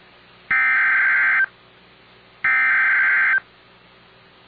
Emergency Alert System (SAME) tones (header bursts), heard as harsh buzzing bursts. It is heard from 0.5 to 3.3 s.